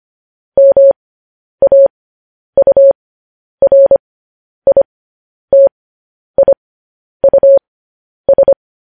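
Morse code, heard as a keyed tone, about 25 wpm, spaced as for 10 wpm, 550 Hz. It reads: MAURITIUS
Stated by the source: Morse code news headlines practice, at 10 WPM.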